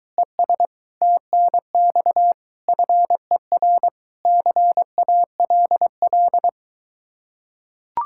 Morse code, heard as a keyed tone, about 23 wpm, 700 Hz. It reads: ES TNX FER CALL